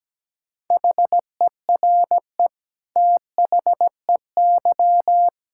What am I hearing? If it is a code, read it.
Morse code, heard as a keyed tone, about 17 wpm, 700 Hz. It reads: HERE THEY
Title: Sentences from Top 1600 Words 17wpm